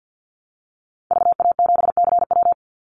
Morse code, heard as a keyed tone, about 49 wpm, 700 Hz. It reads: VA7CW
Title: Call Signs Repeated 2x - 50wpm